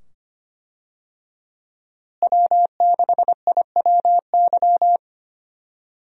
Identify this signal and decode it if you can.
Morse code, heard as a keyed tone, about 25 wpm, 700 Hz. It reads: W6IWY
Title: USA Call Signs - 5 Characters - Repeated 2x - 25wpm